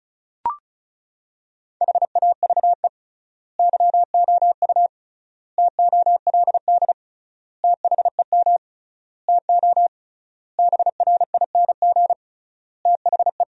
Morse code, heard as a keyed tone, about 36 wpm, 700 Hz. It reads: HAVE YOU TOLD THEM TO BRING THE
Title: 3x Word-Spacing - Sentences from Top 500 Words - 35wpm